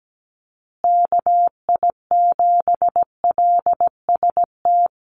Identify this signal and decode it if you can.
Morse code, heard as a keyed tone, about 17 wpm, 700 Hz. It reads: KI7LST
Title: USA Call Signs Repeated 2x 17wpm